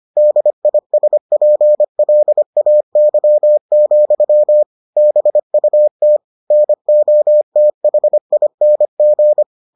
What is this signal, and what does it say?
Morse code, heard as a keyed tone, about 25 wpm, 600 Hz. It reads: DISPLAY, BUT NOTHING